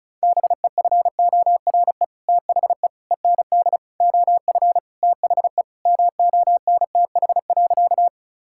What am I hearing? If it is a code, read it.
Morse code, heard as a keyed tone, about 36 wpm, 700 Hz. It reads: BEFORE THE END OF THE MONTH.
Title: Sentences from Top 1000 Words - Review Part 5 - 35wpm